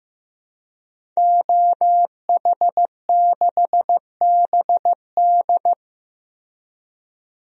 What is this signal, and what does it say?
Morse code, heard as a keyed tone, about 15 wpm, 700 Hz. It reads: OH6BD
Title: DXCC-100 Call Signs - 5 Characters - Repeated 3x - 15wpm